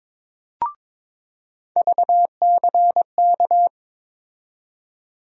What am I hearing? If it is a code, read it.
Morse code, heard as a keyed tone, about 22 wpm, 700 Hz. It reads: VCK
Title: USA Call Sign Postfixes - 3 Letters 22wpm